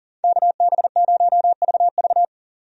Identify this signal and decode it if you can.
Morse code, heard as a keyed tone, about 37 wpm, 700 Hz. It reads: KB0VV
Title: USA Call Signs - 5 Characters - 40wpm